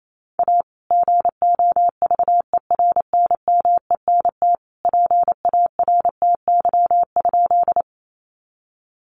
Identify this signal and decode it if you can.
Morse code, heard as a keyed tone, about 28 wpm, 700 Hz. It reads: A GOVERNMENT PARTY?
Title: Sentences from Top 1400 Words 28wpm